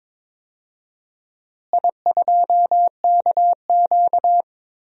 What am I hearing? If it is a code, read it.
Morse code, heard as a keyed tone, about 22 wpm, 700 Hz. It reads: I2KQ
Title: DXCC-100 Call Signs - 4 Characters - Repeated 3x - 22wpm